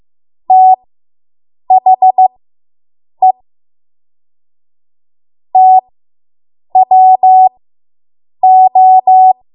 Morse code, heard as a keyed tone, about 15 wpm, spaced as for 7 wpm, 750 Hz. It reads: THE TWO